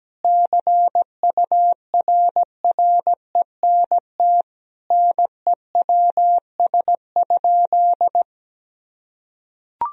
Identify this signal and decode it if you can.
Morse code, heard as a keyed tone, about 17 wpm, 700 Hz. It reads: CURRENT NEWS?
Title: Sentences from Top 1000 Words - Review Part 4 - 17wpm